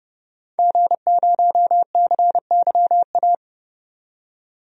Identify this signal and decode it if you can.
Morse code, heard as a keyed tone, about 30 wpm, 700 Hz. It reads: G0CYA
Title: DXCC-100 Call Signs - 5 Characters - 30wpm